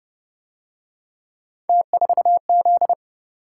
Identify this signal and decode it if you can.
Morse code, heard as a keyed tone, about 28 wpm, 700 Hz. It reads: T4Z